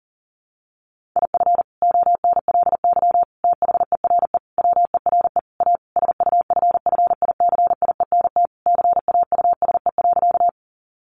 Morse code, heard as a keyed tone, about 40 wpm, 700 Hz. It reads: IF ONLY THERE WERE A SUFFICIENT CAUSE.
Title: Sentences from Top 1400 Words 40wpm